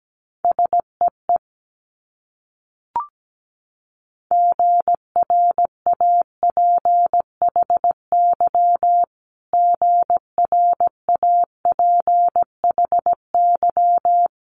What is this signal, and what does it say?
Morse code, heard as a keyed tone, about 17 wpm, 700 Hz. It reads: SEE GRAPHY GRAPHY
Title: Multi-letter Phonemes Repeated 3x 17wpm